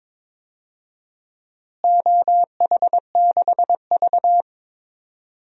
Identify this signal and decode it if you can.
Morse code, heard as a keyed tone, about 21 wpm, 700 Hz. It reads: OH6V
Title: DXCC-100 Call Signs - 4 Characters - Repeated 2x - 22wpm